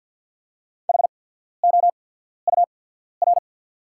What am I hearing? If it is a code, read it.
Morse code, heard as a keyed tone, about 48 wpm, spaced as for 14 wpm, 690 Hz. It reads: HOUR
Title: 8x Character Spacing - Top 500 Words - 50wpm